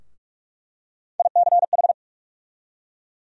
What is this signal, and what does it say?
Morse code, heard as a keyed tone, about 43 wpm, 700 Hz. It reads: ICH
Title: USA Call Sign Postfixes - 3 Letters 45wpm